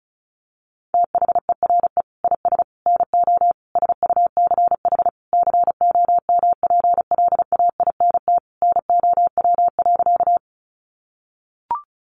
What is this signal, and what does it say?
Morse code, heard as a keyed tone, about 34 wpm, 700 Hz. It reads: THERE IS NO SUCH COMPLAINT NOW.
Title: Sentences from Top 1400 Words 35wpm